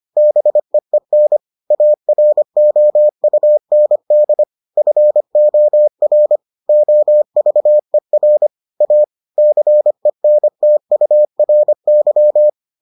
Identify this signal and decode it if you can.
Morse code, heard as a keyed tone, about 25 wpm, 600 Hz. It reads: BEEN AROUND FOR OVER A CENTURY